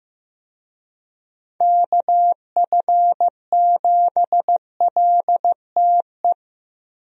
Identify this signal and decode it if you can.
Morse code, heard as a keyed tone, about 15 wpm, 700 Hz. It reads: KF7LTE
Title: USA Call Signs - 6 Characters - Repeated 2x - 15wpm